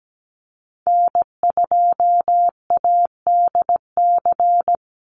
Morse code, heard as a keyed tone, about 17 wpm, 700 Hz. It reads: N2ADC
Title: Call Signs Repeated 3x - 17wpm